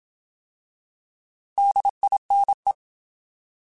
Morse code, heard as a keyed tone, about 27 wpm, 800 Hz. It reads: DINE
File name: Alphabetical / Alphabetical mp3s / Ds @25